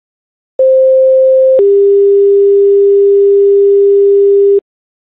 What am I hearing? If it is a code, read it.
Two-tone page, heard as a one-second tone, then a three-second tone, about 520 then 400 Hz. It is heard from 0.6 to 4.6 s.